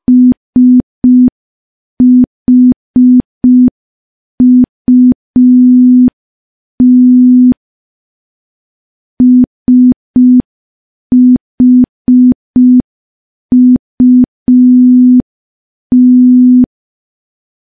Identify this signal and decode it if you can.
Morse code, heard as a keyed tone, about 5 wpm, 250 Hz. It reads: SHUT SHUT
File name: morse / morse-training-05WPM